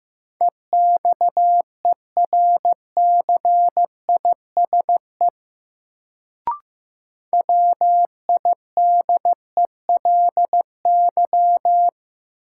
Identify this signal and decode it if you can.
Morse code, heard as a keyed tone, about 15 wpm, 700 Hz. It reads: EXERCISE WIDELY
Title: Sentences from Top 1200 Words - New Word Drill Alt - 15wpm